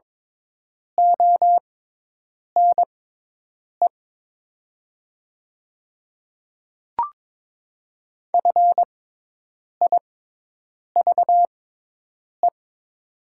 Morse code, heard as a keyed tone, about 22 wpm, spaced as for 8 wpm, 700 Hz. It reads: ONE FIVE
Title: 6x Character Spacing - Top 500 Words - 22wpm